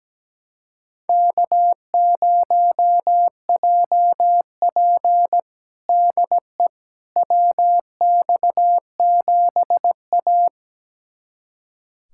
Morse code, heard as a keyed tone, about 17 wpm, 700 Hz. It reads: K0JP DE WX7A
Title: USA Callsign DE Callsign - 4 Characters - 17wpm